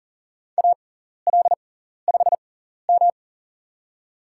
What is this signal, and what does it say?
Morse code, heard as a keyed tone, about 40 wpm, spaced as for 14 wpm, 700 Hz. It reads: AL5M
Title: USA Call Signs - 6x Character Spacing - 40wpm